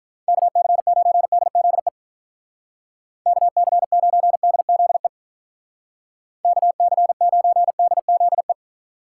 Morse code, heard as a keyed tone, about 43 wpm, 700 Hz. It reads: KC9DZE KC9DZE KC9DZE
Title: Speed Racing - USA Call Signs 35wpm